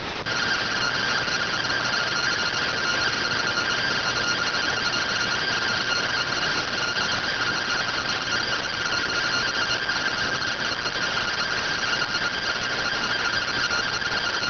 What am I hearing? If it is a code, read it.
Two-tone teleprinter signal, about 170 Hz shift.